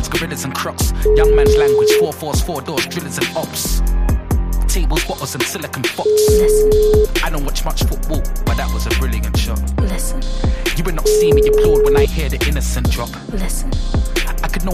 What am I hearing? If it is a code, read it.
Ringback tone (Europe, 425 Hz), with speech.